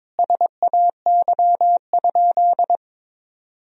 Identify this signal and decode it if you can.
Morse code, heard as a keyed tone, about 22 wpm, 700 Hz. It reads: SAY?
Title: Sentences from Top 1000 Words - Review Part 1 - 22wpm